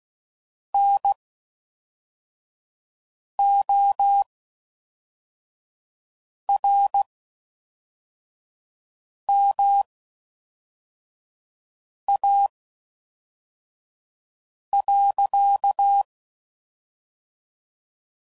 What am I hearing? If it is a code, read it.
Morse code, heard as a keyed tone, about 16 wpm, spaced as for 4 wpm, 800 Hz. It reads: NORMA.